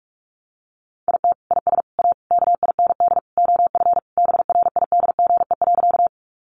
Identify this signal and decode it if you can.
Morse code, heard as a keyed tone, about 44 wpm, 700 Hz. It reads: IT IS A KIND OF BRIDGE.
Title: Sentences from Top 1200 Words 45wpm